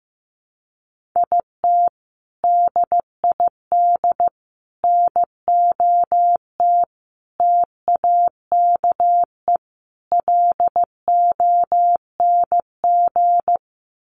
Morse code, heard as a keyed tone, about 15 wpm, 700 Hz. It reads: IT DID NOT TAKE LONG
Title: Sets of 5 Words Encore2 15wpm